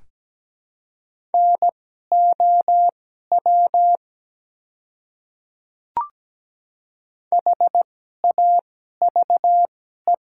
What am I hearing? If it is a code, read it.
Morse code, heard as a keyed tone, about 17 wpm, spaced as for 12 wpm, 690 Hz. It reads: NOW HAVE
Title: 2x Character Spacing - Top 100 Words - 17wpm